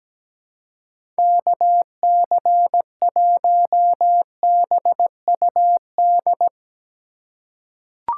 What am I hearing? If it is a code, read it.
Morse code, heard as a keyed tone, about 17 wpm, 700 Hz. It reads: KC1BUD
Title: USA Call Signs - 6 Characters - 4x Character Spacing - 17wpm